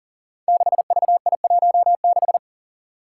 Morse code, heard as a keyed tone, about 39 wpm, 700 Hz. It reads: 6VI16
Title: Five Challenging Letter-Number Combinations 40wpm